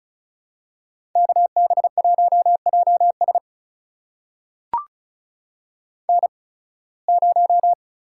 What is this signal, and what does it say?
Morse code, heard as a keyed tone, about 36 wpm, spaced as for 23 wpm, 700 Hz. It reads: KB1JS N0
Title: USA Call Signs - 5 Characters - 8x Character Spacing - 35wpm